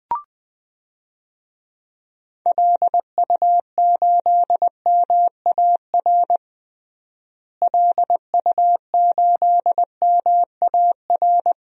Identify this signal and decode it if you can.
Morse code, heard as a keyed tone, about 20 wpm, 700 Hz. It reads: LU8MAR LU8MAR
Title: DXCC-100 Call Signs - 6 Characters - Repeated 2x - 20wpm